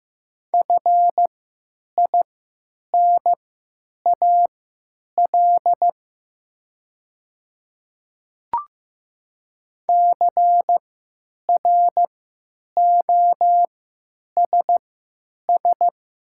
Morse code, heard as a keyed tone, about 15 wpm, spaced as for 9 wpm, 700 Hz. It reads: FINAL CROSS
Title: 3x Character Spacing - Top 500 Words - 15wpm